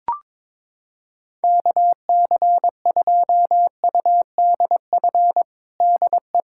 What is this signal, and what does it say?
Morse code, heard as a keyed tone, about 22 wpm, 700 Hz. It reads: KC2UDF DE
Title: USA Callsign DE Callsign - 6 Characters - 22wpm